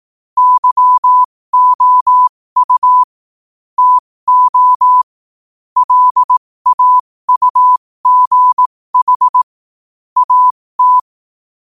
Morse code, heard as a keyed tone, about 18 wpm, spaced as for 15 wpm, 1000 Hz. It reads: YOU TO LAUGH AT
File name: 15wpm